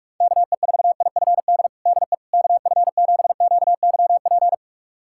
Morse code, heard as a keyed tone, about 46 wpm, 700 Hz. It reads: KE4IFD DE KF7QYP
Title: USA Callsign DE Callsign - 6 Characters - 45wpm